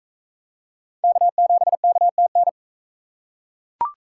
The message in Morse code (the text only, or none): K7KTN